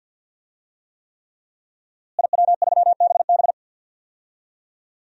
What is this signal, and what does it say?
Morse code, heard as a keyed tone, about 51 wpm, 700 Hz. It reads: IW3BB
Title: DXCC-100 Call Signs - 5 Characters - Repeated 3x - 50wpm